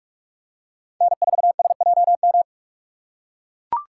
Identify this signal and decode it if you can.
Morse code, heard as a keyed tone, about 44 wpm, 700 Hz. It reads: N4SJM